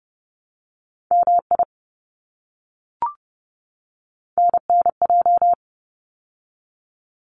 Morse code, heard as a keyed tone, about 30 wpm, 700 Hz. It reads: MI NNJ
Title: ARRL Sections 30wpm